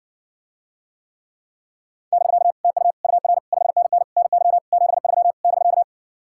Morse code, heard as a keyed tone, about 60 wpm, 700 Hz. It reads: <BK> TU UR 5NN NY 73 <BK>